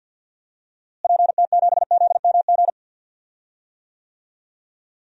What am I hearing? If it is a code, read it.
Morse code, heard as a keyed tone, about 51 wpm, 700 Hz. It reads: PT7ZMG
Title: DXCC-100 Call Signs - 6 Characters - Repeated 2x - 50wpm